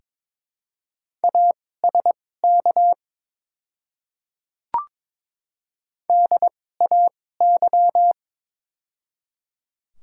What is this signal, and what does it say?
Morse code, heard as a keyed tone, about 22 wpm, spaced as for 16 wpm, 700 Hz. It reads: ASK DAY
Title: 2x Character Spacing - Top 300 Words - 22wpm